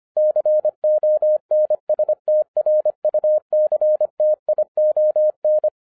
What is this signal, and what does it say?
Morse code, heard as a keyed tone, about 24 wpm, 610 Hz. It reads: CONSTRUCTION
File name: 2011-10-mp3 / n04-JUST MOVED 25wpm